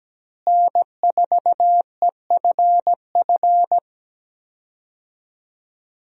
Morse code, heard as a keyed tone, about 17 wpm, 700 Hz. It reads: N4EFF
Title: USA Call Signs - 5 Characters - Repeated 3x - 17wpm